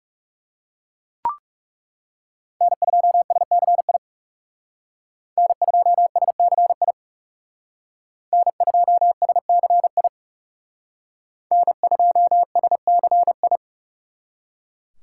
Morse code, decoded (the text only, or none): N2SCI N2SCI N2SCI N2SCI